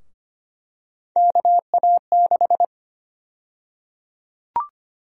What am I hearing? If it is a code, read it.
Morse code, heard as a keyed tone, about 25 wpm, 700 Hz. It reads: KA6
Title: USA Call Sign Prefixes - Repeated 2x - 25wpm